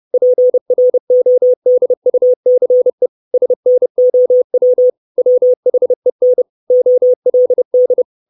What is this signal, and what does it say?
Morse code, heard as a keyed tone, about 30 wpm, 500 Hz. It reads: PRODUCE SNOW WHEN OLD
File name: 632 Words 30WPM